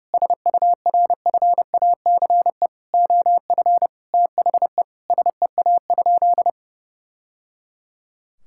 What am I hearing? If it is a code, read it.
Morse code, heard as a keyed tone, about 28 wpm, 700 Hz. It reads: SURFACE OF THE SEA?
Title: Sentences from Top 1000 Words - Review Part 5 - 30wpm